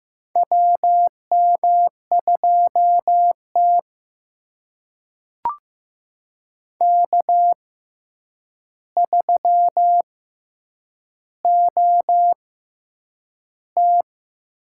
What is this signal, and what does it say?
Morse code, heard as a keyed tone, about 15 wpm, spaced as for 8 wpm, 700 Hz. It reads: WM2T K3OT
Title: USA Call Signs - 4 Characters - 6x Character Spacing - 15wpm